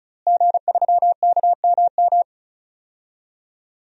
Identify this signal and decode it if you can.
Morse code, heard as a keyed tone, about 36 wpm, 700 Hz. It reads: G3KMM
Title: DXCC-100 Call Signs - 5 Characters - Repeated 3x - 35wpm